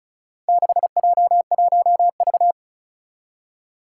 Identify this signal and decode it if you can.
Morse code, heard as a keyed tone, about 36 wpm, 700 Hz. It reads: 6J1V
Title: Four Challenging Letter-Number Combinations 35wpm